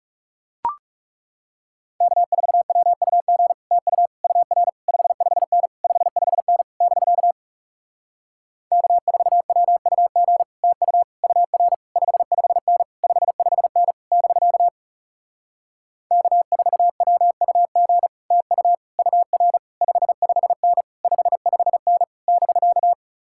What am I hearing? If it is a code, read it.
Morse code, heard as a keyed tone, about 37 wpm, 700 Hz. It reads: K4WUG TU UR 55N 55N <BK> K4WUG TU UR 55N 55N <BK> K4WUG TU UR 55N 55N <BK>
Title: Speed-Racing Common QSO Phrases - 30wpm